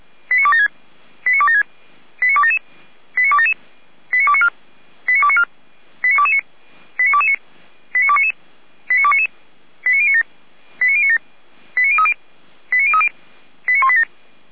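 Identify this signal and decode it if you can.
Five-tone selective call (ZVEI), heard as quick runs of five tones, tones of about 70 ms. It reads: 89276 89276 89277 89277 89294 89294 89229 89229 89299 89299 89097 89097 89030 89030 89187
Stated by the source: ZVEI five-tone selective call 89265, received by SDR.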